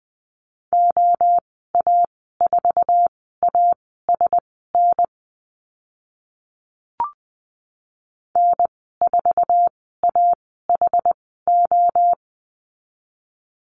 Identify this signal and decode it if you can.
Morse code, heard as a keyed tone, about 20 wpm, spaced as for 14 wpm, 700 Hz. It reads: OA4ASN N4AHO
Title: 062 Lesson 11g - Letter h - Call Signs 20wpm